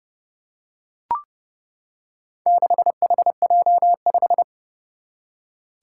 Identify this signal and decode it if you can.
Morse code, heard as a keyed tone, about 31 wpm, 700 Hz. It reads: 6HJ5
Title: Four Challenging Letter-Number Combinations 30wpm